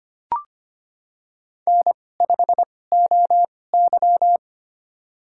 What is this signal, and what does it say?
Morse code, heard as a keyed tone, about 25 wpm, spaced as for 18 wpm, 700 Hz. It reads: N5OY
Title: USA Call Signs - 4 Characters - 2x Character Spacing - 25wpm